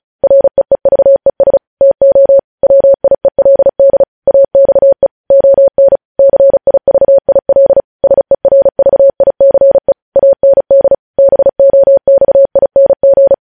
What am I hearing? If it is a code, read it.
Morse code, heard as a keyed tone, about 36 wpm, 550 Hz. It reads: REEVES TO WIELD AXE ON CIVIL SERVICE AND BOXING
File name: podcast / News headlines-1x-35x35